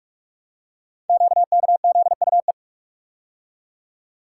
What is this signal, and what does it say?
Morse code, heard as a keyed tone, about 46 wpm, 700 Hz. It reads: QKZUE
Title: Code Groups - 5 Letters - 45wpm